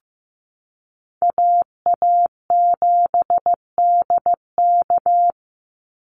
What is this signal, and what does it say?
Morse code, heard as a keyed tone, about 15 wpm, 700 Hz. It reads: AA7DK
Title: NA SOTA Chaser Callsigns 15wpm